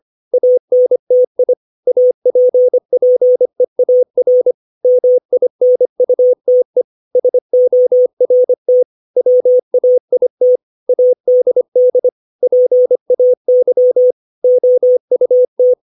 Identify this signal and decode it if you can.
Morse code, heard as a keyed tone, about 25 wpm, 500 Hz. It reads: ANTI APPEAR MINUTE SORT WAIT ADD PAY OUT
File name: 632 words / 632 Words 25WPM